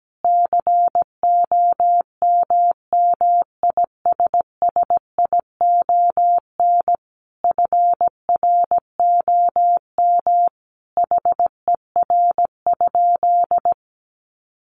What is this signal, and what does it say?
Morse code, heard as a keyed tone, about 17 wpm, 700 Hz. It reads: COMMISSION FROM HER?
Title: Sentences from Top 1400 Words 17wpm